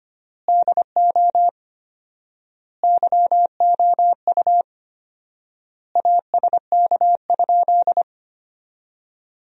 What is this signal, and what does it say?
Morse code, heard as a keyed tone, about 25 wpm, 700 Hz. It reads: DO YOU ASK?
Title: 4x Word-Spacing - Sentences from Top 200 Words - 25wpm